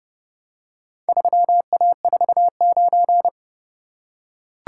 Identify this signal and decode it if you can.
Morse code, heard as a keyed tone, about 29 wpm, 700 Hz. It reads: 3A49